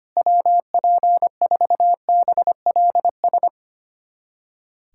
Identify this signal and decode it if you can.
Morse code, heard as a keyed tone, about 25 wpm, 700 Hz. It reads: WP4BLS
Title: USA Callsign DE Callsign - 6 Characters - 25wpm